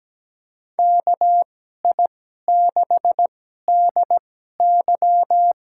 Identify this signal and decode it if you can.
Morse code, heard as a keyed tone, about 17 wpm, spaced as for 12 wpm, 700 Hz. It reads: KI6DY
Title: USA Call Signs - 2x Character Spacing - 17wpm